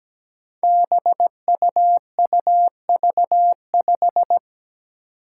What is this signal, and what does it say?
Morse code, heard as a keyed tone, about 17 wpm, 700 Hz. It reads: BUUV5
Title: Five Challenging Letter-Number Combinations 17wpm